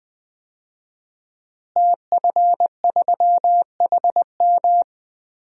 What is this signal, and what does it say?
Morse code, heard as a keyed tone, about 20 wpm, 700 Hz. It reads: TF3HM
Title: DXCC-100 Call Signs - 5 Characters - Repeated 2x - 20wpm